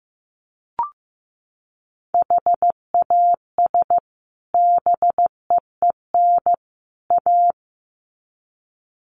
Morse code, heard as a keyed tone, about 15 wpm, 700 Hz. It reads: HAS BEEN A
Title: Sets of 3 Words 15wpm